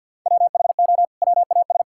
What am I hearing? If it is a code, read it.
Morse code, heard as a keyed tone, about 50 wpm, 700 Hz. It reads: WHO WAS
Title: Sentences from Top 1000 Words - Review Part 5 - 50wpm